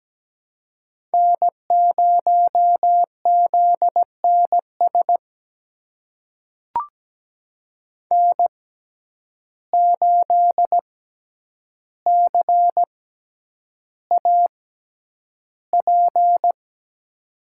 Morse code, heard as a keyed tone, about 17 wpm, spaced as for 9 wpm, 700 Hz. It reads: N0ZNS N8CAP
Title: USA Call Signs - 5 Characters - 6x Character Spacing - 17wpm